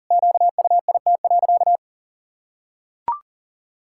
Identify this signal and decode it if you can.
Morse code, heard as a keyed tone, about 40 wpm, 700 Hz. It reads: QUIT.
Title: Pangram Challenge v2 - 40wpm